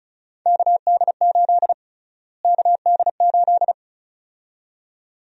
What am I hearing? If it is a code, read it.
Morse code, heard as a keyed tone, about 36 wpm, 700 Hz. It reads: KD8 KD8
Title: USA Call Sign Prefixes - Repeated 2x - 35wpm